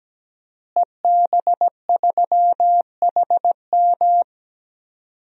Morse code, heard as a keyed tone, about 17 wpm, 700 Hz. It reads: EB3HM
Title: DXCC-100 Call Signs - 5 Characters - Repeated 2x - 17wpm